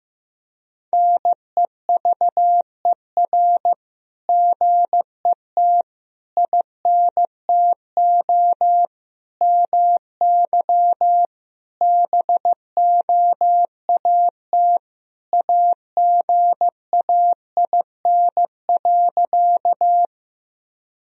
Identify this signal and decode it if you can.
Morse code, heard as a keyed tone, about 15 wpm, 700 Hz. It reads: NEVER GET INTO MY BOAT AGAIN.
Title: Sentences from Top 1000 Words - Review Part 3 - 15wpm